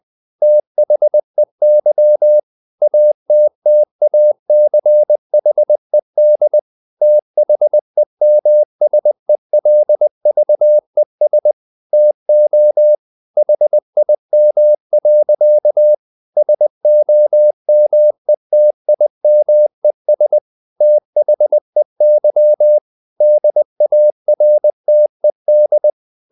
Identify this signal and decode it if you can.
Morse code, heard as a keyed tone, about 20 wpm, 600 Hz. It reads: THEY ATTACHED THEMSELVES TO HIM. SOMETIMES THEY DARTED
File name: the old man and the sea part 2 20wpm